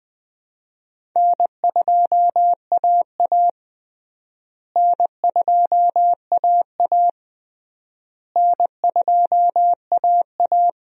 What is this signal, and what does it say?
Morse code, heard as a keyed tone, about 21 wpm, 700 Hz. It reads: N2AA N2AA N2AA
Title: USA Call Signs - 4 Characters - Repeated 3x - 20wpm